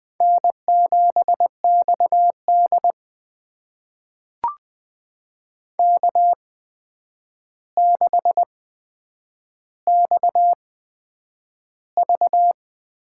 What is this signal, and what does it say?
Morse code, heard as a keyed tone, about 20 wpm, spaced as for 9 wpm, 700 Hz. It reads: N7XD K6XV
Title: USA Call Signs - 4 Characters - 8x Character Spacing - 20wpm